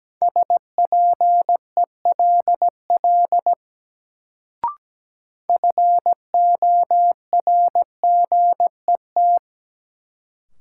Morse code, heard as a keyed tone, about 17 wpm, 700 Hz. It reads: SPELL FORGET
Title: Sentences from Top 1200 Words - New Word Drill Alt - 17wpm